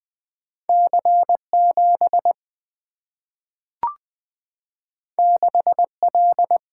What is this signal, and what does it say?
Morse code, heard as a keyed tone, about 20 wpm, 700 Hz. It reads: C7 6L
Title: Single-Letter Single-Number 20wpm